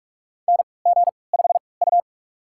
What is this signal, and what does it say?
Morse code, heard as a keyed tone, about 46 wpm, spaced as for 26 wpm, 700 Hz. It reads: NG5U
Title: USA Call Signs - 4 Characters - 3x Character Spacing - 45wpm